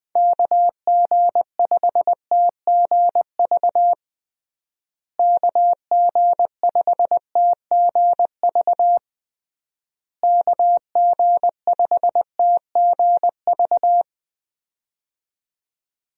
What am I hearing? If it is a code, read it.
Morse code, heard as a keyed tone, about 20 wpm, 700 Hz. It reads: KG5TGV KG5TGV KG5TGV
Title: USA Call Signs - 6 Characters - Repeated 3x - 20wpm